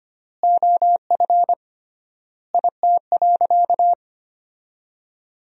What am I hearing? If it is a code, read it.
Morse code, heard as a keyed tone, about 25 wpm, 700 Hz. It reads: OF IT.
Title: 3x Word-Spacing - Sentences from Top 400 Words - 25wpm